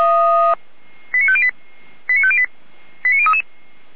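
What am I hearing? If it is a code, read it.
Five-tone selective call (ZVEI), heard as quick runs of five tones, tones of about 70 ms. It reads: 89508 89508 89033
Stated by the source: ZVEI five-tone selective call 89058, SDR recording, Feuerwehr Orlamünde.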